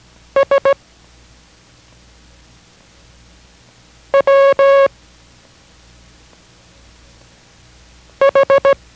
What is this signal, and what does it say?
Morse code, heard as a keyed tone, about 16 wpm, 550 Hz. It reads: S W H